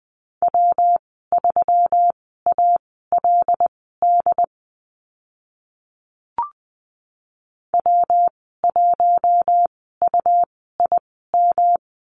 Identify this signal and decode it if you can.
Morse code, heard as a keyed tone, about 20 wpm, spaced as for 15 wpm, 700 Hz. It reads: W3ALD W1UIM
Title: 147 Lesson 24g - Letter y - Call Signs 20wpm